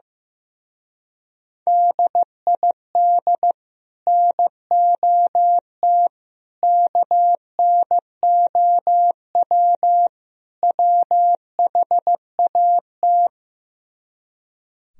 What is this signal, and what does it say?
Morse code, heard as a keyed tone, about 15 wpm, 700 Hz. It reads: DID NOT KNOW WHAT
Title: Speed-Racing - Sets of 4 Words 15wpm